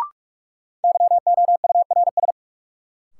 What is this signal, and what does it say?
Morse code, heard as a keyed tone, about 46 wpm, 700 Hz. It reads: YOURS
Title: Sentences from Top 1500 Words - New Word Drill - 45wpm